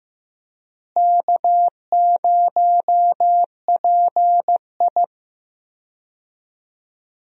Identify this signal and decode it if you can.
Morse code, heard as a keyed tone, about 15 wpm, 700 Hz. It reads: K0PI
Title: USA Call Signs - 4 Characters - Repeated 3x - 15wpm